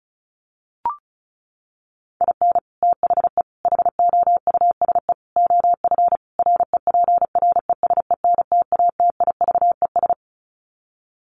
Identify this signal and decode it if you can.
Morse code, heard as a keyed tone, about 34 wpm, 700 Hz. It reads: IN THE HOUSE OF REPRESENTATIVES
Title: Sets of 5 Words Encore 35wpm